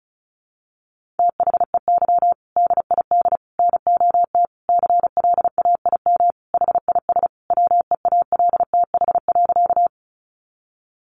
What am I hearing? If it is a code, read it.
Morse code, heard as a keyed tone, about 34 wpm, 700 Hz. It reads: THEY DID NOT CLAIM HIS WEALTH.
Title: Sentences from Top 1500 Words 35wpm